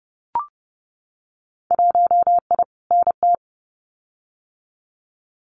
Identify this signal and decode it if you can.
Morse code, heard as a keyed tone, about 30 wpm, 700 Hz. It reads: 1I NT
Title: Winter Field Day Exchanges Encore 30wpm